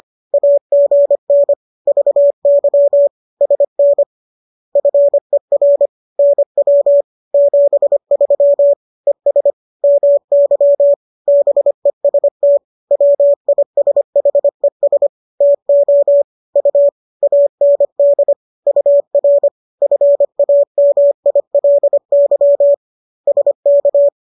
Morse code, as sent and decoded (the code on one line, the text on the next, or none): .- --. -. / ...- -.-- / ... -. / ..-. .-. / -. .-- / --... ...-- / . ... / -- -.-- / -... ... - / .-- .. ... .... ... / - --- / ..- / .- -. -.. / ..- .-. / ..-. .- -- .. .-.. -.-- / ... -.-
AGN VY SN FER NW 73 ES MY BEST WISHES TO U AND UR FAMILY SK